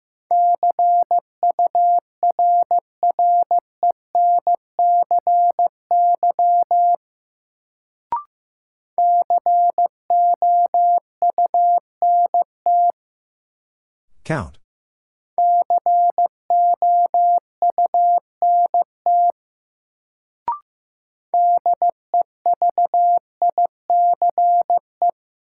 Morse code, heard as a keyed tone, about 15 wpm, 700 Hz. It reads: CURRENCY COUNT COUNT DEVICE
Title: Sentences from Top 1500 Words - New Word Drill Alt - 15wpm